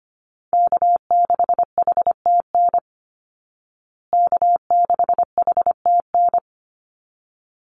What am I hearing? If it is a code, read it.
Morse code, heard as a keyed tone, about 25 wpm, 700 Hz. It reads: K6HTN K6HTN
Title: Call Signs Repeated 3x - 25wpm